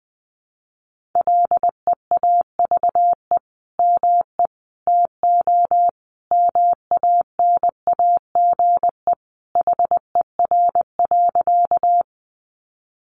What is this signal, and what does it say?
Morse code, heard as a keyed tone, about 20 wpm, 700 Hz. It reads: LEAVE ME TO MANAGE HER.
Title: Sentences from Top 1400 Words 20wpm